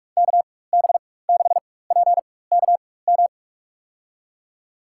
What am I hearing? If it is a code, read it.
Morse code, heard as a keyed tone, about 41 wpm, spaced as for 20 wpm, 700 Hz. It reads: KB6PKM
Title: USA Call Signs - 6 Characters - 4x Character Spacing - 45wpm